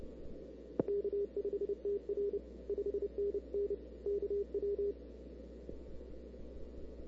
Morse code, heard as a keyed tone, about 29 wpm, 400 Hz. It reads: K5TR 5NN KW
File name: CW / 2005-k5tr arrl cw dx -00:00